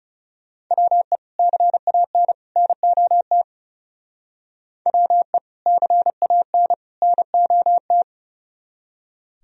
Morse code, heard as a keyed tone, about 39 wpm, 700 Hz. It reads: WE CAN NOT WE CAN NOT